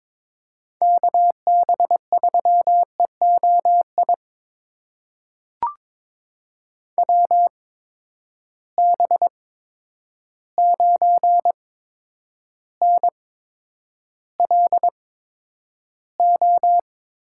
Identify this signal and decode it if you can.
Morse code, heard as a keyed tone, about 21 wpm, spaced as for 9 wpm, 700 Hz. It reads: KB3EOI WB9NLO
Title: USA Call Signs - 6 Characters - 8x Character Spacing - 22wpm